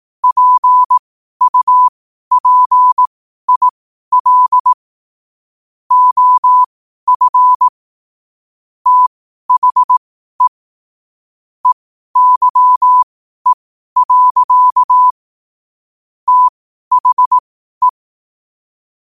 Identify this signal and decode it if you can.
Morse code, heard as a keyed tone, about 18 wpm, spaced as for 12 wpm, 1000 Hz. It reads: PUPIL OF THE EYE. THE
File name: qotdfiles / 12wpm